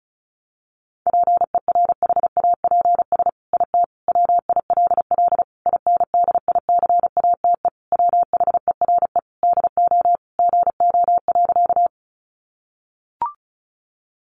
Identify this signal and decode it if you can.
Morse code, heard as a keyed tone, about 34 wpm, 700 Hz. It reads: PERHAPS IT WILL INDICATE WHERE DO GO.
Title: Sentences from Top 1300 Words 35wpm